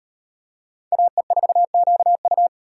Morse code, heard as a keyed tone, about 35 wpm, 700 Hz. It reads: AE4QU